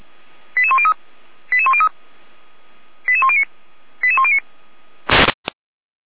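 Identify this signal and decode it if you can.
Five-tone selective call (ZVEI), heard as quick runs of five tones, tones of about 70 ms. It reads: 88103 88103 88109 88109
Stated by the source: Received by SDR.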